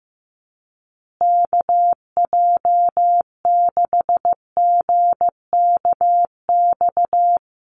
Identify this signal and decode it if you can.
Morse code, heard as a keyed tone, about 15 wpm, 700 Hz. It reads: KJ6GKX
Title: USA Call Signs Repeated 3x 15wpm